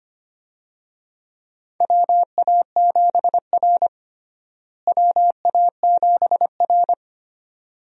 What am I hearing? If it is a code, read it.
Morse code, heard as a keyed tone, about 25 wpm, 700 Hz. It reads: WA7R WA7R
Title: USA Call Signs - 4 Characters - Repeated 2x - 25wpm